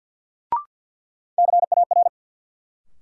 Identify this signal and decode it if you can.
Morse code, heard as a keyed tone, about 51 wpm, 700 Hz. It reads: CAR
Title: Sets of 5 Words Encore - New Word Drill - 50wpm